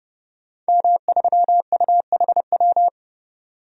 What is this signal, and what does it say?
Morse code, heard as a keyed tone, about 30 wpm, 700 Hz. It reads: M3UHW
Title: DXCC-100 Call Signs - 5 Characters - 30wpm